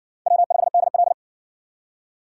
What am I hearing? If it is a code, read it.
Morse code, heard as a keyed tone, about 60 wpm, 700 Hz. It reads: W5DL